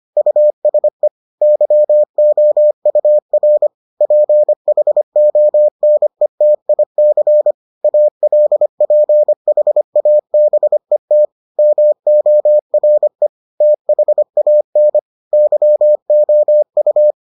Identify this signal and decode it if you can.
Morse code, heard as a keyed tone, about 25 wpm, 600 Hz. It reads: USE YOUR PHONETIC ALPHABET MORE THAN YOU